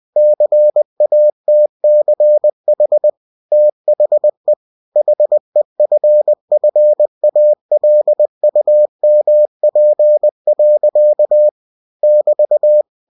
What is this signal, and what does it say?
Morse code, heard as a keyed tone, about 20 wpm, 600 Hz. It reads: CATCH THE HEFFALUMP. <BT>
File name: cw-books / winnie-the-pooh / winnie-20wpm-page-0039